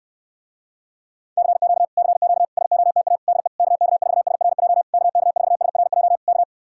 Morse code, heard as a keyed tone, about 69 wpm, 700 Hz. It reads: CQ CQ SOTA DE KK4SRJ KK4SRJ K